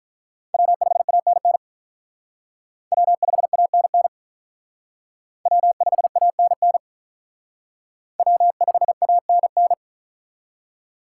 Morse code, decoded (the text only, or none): W5ANN W5ANN W5ANN W5ANN